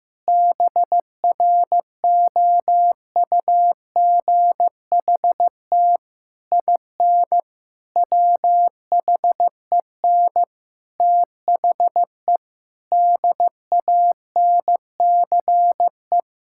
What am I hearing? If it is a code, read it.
Morse code, heard as a keyed tone, about 15 wpm, 700 Hz. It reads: BROUGHT IN WHEN THE DANCE
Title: Sentences from Top 1000 Words - Review Part 6 - 15wpm